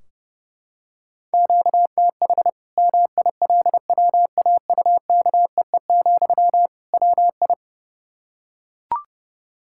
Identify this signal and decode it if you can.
Morse code, heard as a keyed tone, about 30 wpm, 700 Hz. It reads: QTH MILWAUKEE, WI
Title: Practice QSO - Exchange 1 - Beginning - 30wpm